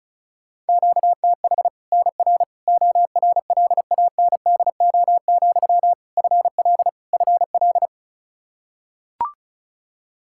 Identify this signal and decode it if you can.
Morse code, heard as a keyed tone, about 36 wpm, 700 Hz. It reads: QTH NR ORLANDO, FL FL
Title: Practice QSO - Exchange 1 - Beginning - 35wpm